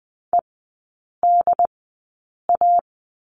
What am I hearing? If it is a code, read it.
Morse code, heard as a keyed tone, about 20 wpm, spaced as for 8 wpm, 690 Hz. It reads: EDA
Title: 145 Lesson 24e - Letter y - Three Characters 20wpm